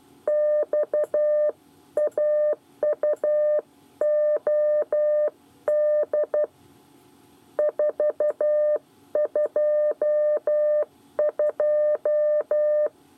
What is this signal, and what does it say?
Morse code, heard as a keyed tone, about 12 wpm, 600 Hz. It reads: XAUOD 422
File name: 8 bis lezione cw 1